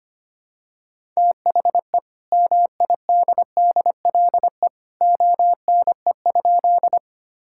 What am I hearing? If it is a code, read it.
Morse code, heard as a keyed tone, about 25 wpm, 700 Hz. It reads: THE MIDDLE ONE?
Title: Sentences from Top 1000 Words - Review Part 6 - 25wpm